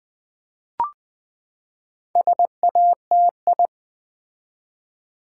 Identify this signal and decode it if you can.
Morse code, heard as a keyed tone, about 20 wpm, 700 Hz. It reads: SATI
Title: Build A Word - 4 Letters At A Time - 12-Letter Words - 20wpm